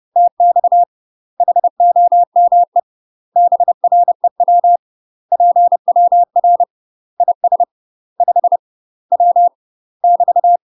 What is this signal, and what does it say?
Morse code, heard as a keyed tone, about 30 wpm, 700 Hz. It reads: TX HOME BREW PWR IS 5 W <BT>